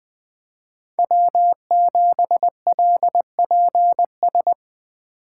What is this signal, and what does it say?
Morse code, heard as a keyed tone, about 20 wpm, 700 Hz. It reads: W7LPS